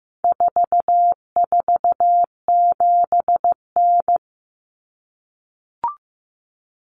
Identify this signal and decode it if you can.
Morse code, heard as a keyed tone, about 15 wpm, 700 Hz. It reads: 447N
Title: Common Cut Numbers - 4-Digit 15wpm